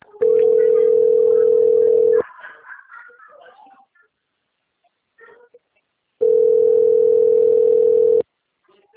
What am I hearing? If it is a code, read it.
Ringback tone (North America), 440 + 480 Hz.